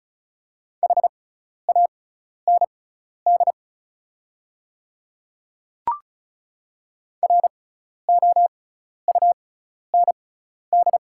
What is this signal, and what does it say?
Morse code, heard as a keyed tone, about 36 wpm, spaced as for 12 wpm, 690 Hz. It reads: HAND ROUND
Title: 6x Character Spacing - Top 500 Words - 35wpm